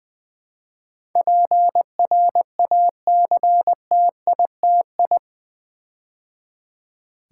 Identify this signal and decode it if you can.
Morse code, heard as a keyed tone, about 20 wpm, 700 Hz. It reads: PRACTITI
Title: Build A Word - 4 Letters At A Time - 12-Letter Words - 20wpm